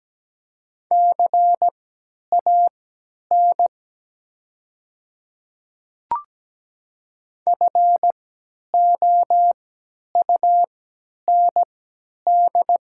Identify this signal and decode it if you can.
Morse code, heard as a keyed tone, about 17 wpm, spaced as for 10 wpm, 700 Hz. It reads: CAN FOUND
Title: 3x Character Spacing - Top 300 Words - 17wpm